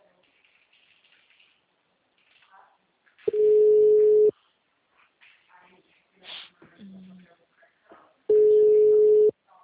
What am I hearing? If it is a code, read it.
Ringback tone (Europe, 425 Hz), with speech.